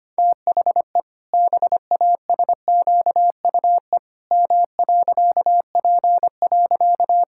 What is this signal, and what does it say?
Morse code, heard as a keyed tone, about 25 wpm, 700 Hz. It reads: THE BASQUE M.P.
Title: Pangram Challenge v2 - 25wpm